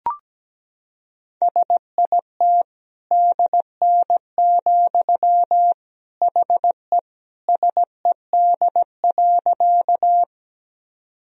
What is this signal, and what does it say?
Morse code, heard as a keyed tone, about 17 wpm, 700 Hz. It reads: SIT DN, HE SED.
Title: Sentences from Top 500 Words and Abbreviations 17wpm